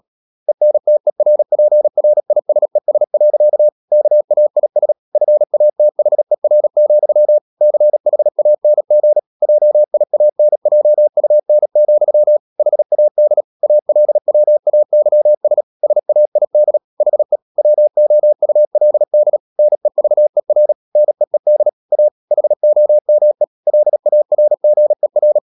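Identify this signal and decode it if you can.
Morse code, heard as a keyed tone, about 37 wpm, 600 Hz. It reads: ENTERPRISES. KAIS FATHER, CHANG JIANJUN, HAD ALWAYS SAID HE WOULD NEVER NEED A HOME LARGER
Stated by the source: Morse code practice at 35–39 wpm sending English text of Ken Follett's novel Never, from CW med Gnister.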